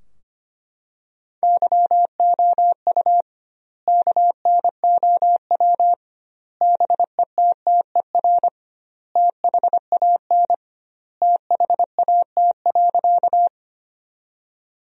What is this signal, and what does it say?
Morse code, heard as a keyed tone, about 25 wpm, 700 Hz. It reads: YOU KNOW BETTER THAN THAT.
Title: 2x Word-Spacing - Sentences from Top 400 Words - 25wpm